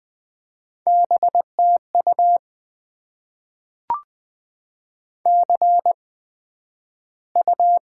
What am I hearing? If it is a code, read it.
Morse code, heard as a keyed tone, about 20 wpm, spaced as for 11 wpm, 700 Hz. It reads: BTU CU
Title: Top 20 QSO Elements - 8x Character Spacing - 20wpm